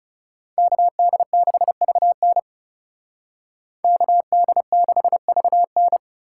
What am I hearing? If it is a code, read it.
Morse code, heard as a keyed tone, about 38 wpm, 700 Hz. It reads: KD6VN KD6VN